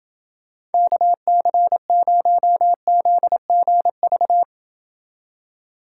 Morse code, heard as a keyed tone, about 30 wpm, 700 Hz. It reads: KC0ZGV